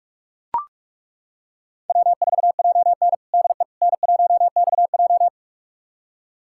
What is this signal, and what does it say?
Morse code, heard as a keyed tone, about 42 wpm, 700 Hz. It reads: W4JN DE N1XJ
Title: USA Callsign DE Callsign - 4 Characters - 45wpm